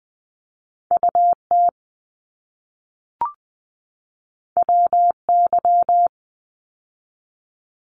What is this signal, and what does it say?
Morse code, heard as a keyed tone, about 20 wpm, 700 Hz. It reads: UT WY
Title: ARRL Sections 20wpm